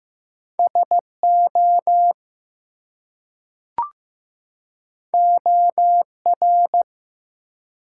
Morse code, heard as a keyed tone, about 15 wpm, 700 Hz. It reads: SO OR
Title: Two Letter Words - 15wpm